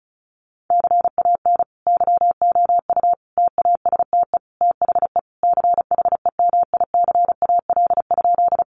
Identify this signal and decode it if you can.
Morse code, heard as a keyed tone, about 34 wpm, 700 Hz. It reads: CAN YOU TASTE THE CHEMICAL?
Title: Sentences from Top 1200 Words 35wpm